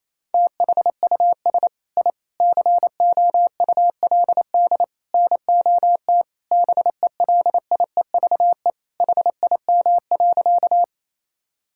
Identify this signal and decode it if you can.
Morse code, heard as a keyed tone, about 28 wpm, 700 Hz. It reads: THUS I COULD NOT BELIEVE HIM.
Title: Sentences from Top 1000 Words - Review Part 5 - 28wpm